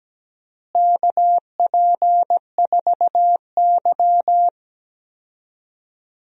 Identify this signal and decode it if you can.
Morse code, heard as a keyed tone, about 17 wpm, 700 Hz. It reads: KP4Y
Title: DXCC-100 Call Signs - 4 Characters - Repeated 2x - 17wpm